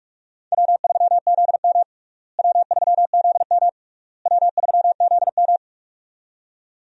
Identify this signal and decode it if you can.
Morse code, heard as a keyed tone, about 46 wpm, 700 Hz. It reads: W3ZM W3ZM W3ZM
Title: USA Call Signs - 4 Characters - Repeated 3x - 45wpm